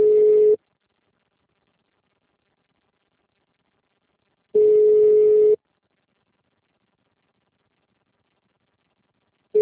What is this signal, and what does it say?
Ringback tone (Europe, 425 Hz).